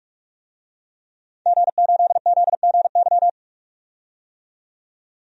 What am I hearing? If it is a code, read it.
Morse code, heard as a keyed tone, about 46 wpm, 700 Hz. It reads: G8ZGY
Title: DXCC-100 Call Signs - 5 Characters - 45wpm